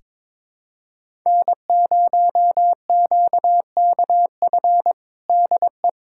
Morse code, heard as a keyed tone, about 22 wpm, 700 Hz. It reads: N0QKF DE
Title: USA Callsign DE Callsign - 5 Characters - 22wpm